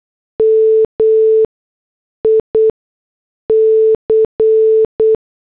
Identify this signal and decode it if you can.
Morse code, heard as a keyed tone, about 8 wpm, 440 Hz. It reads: M I C
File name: morse / headlines